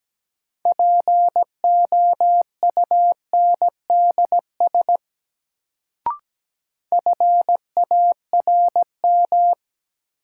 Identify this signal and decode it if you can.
Morse code, heard as a keyed tone, about 17 wpm, 700 Hz. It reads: POUNDS FARM
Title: Sentences from Top 1000 Words - Review Part 2 - New Word Drill - 17wpm